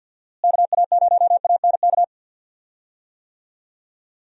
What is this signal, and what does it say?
Morse code, heard as a keyed tone, about 50 wpm, 700 Hz. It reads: KA0ANK
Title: USA Call Signs - 6 Characters - Repeated 3x - 50wpm